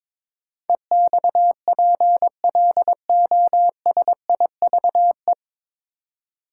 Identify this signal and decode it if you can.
Morse code, heard as a keyed tone, about 22 wpm, 690 Hz. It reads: EXPLOSIVE
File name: 9 Letter Words - First 3 Letters Given - 22wpm